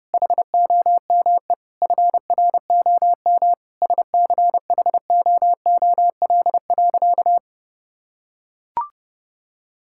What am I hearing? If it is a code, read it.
Morse code, heard as a keyed tone, about 30 wpm, 700 Hz. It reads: HOME FROM SCHOOL.
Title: Sentences from Top 1000 Words - Review Part 6 - 30wpm